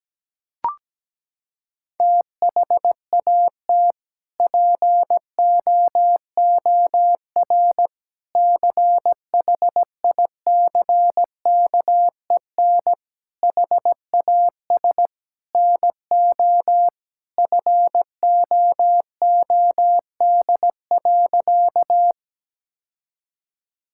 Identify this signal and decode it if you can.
Morse code, heard as a keyed tone, about 17 wpm, 700 Hz. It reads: THAT POOR CHICKEN HAS NO FOOD.
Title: Sentences from Top 1000 Words - Review Part 5 - 17wpm